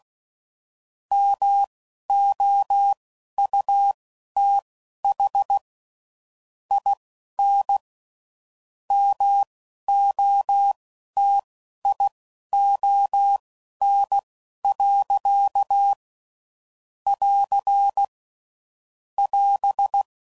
Morse code, heard as a keyed tone, about 16 wpm, spaced as for 11 wpm, 800 Hz. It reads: MOUTH IN MOTION. <AR> <AS>